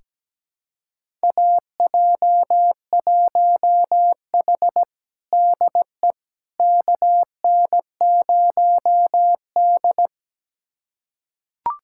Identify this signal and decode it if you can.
Morse code, heard as a keyed tone, about 17 wpm, 700 Hz. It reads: AJ1H DE KN0D
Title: USA Callsign DE Callsign - 4 Characters - 17wpm